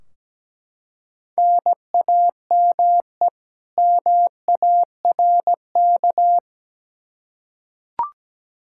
Morse code, heard as a keyed tone, about 17 wpm, 700 Hz. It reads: NAME MARK
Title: Practice QSO - Exchange 1 - Beginning - 17wpm